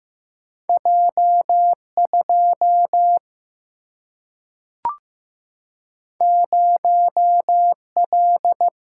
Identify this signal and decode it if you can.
Morse code, heard as a keyed tone, about 15 wpm, 700 Hz. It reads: J 2 0 L
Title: Single-Letter Single-Number 15wpm